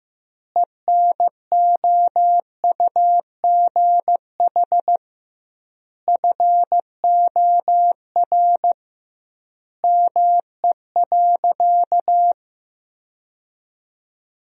Morse code, heard as a keyed tone, about 15 wpm, 700 Hz. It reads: ENOUGH FOR ME.
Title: 2x Word-Spacing - Sentences from Top 400 Words - 15wpm